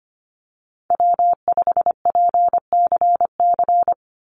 Morse code, heard as a keyed tone, about 25 wpm, 700 Hz. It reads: W5PCC
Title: USA Call Signs 25wpm